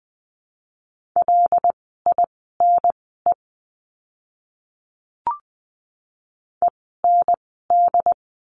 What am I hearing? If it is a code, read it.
Morse code, heard as a keyed tone, about 20 wpm, spaced as for 15 wpm, 700 Hz. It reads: LINE END
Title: 075 Lesson 13f - Letter L - Words 20wpm